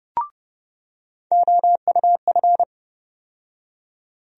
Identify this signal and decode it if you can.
Morse code, heard as a keyed tone, about 30 wpm, 700 Hz. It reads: OUF
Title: USA Call Sign Postfixes - 3 Letters 30wpm